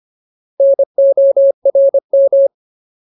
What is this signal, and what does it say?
Morse code, heard as a keyed tone, about 25 wpm, 550 Hz. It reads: NORM
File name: Ultimate Word List / 25WPM Ultimate